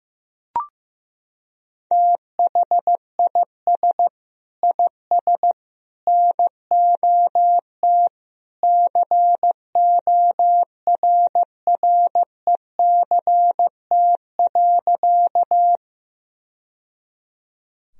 Morse code, heard as a keyed tone, about 15 wpm, 700 Hz. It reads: THIS IS NOT CORRECT.
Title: Sentences from Top 1000 Words - Review Part 2 - 15wpm